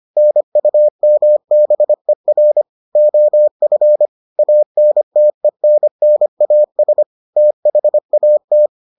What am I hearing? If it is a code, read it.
Morse code, heard as a keyed tone, about 25 wpm, 600 Hz. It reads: NUMBER OF ANTENNAS THAT